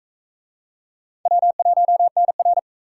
Morse code, heard as a keyed tone, about 40 wpm, 700 Hz. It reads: W1NR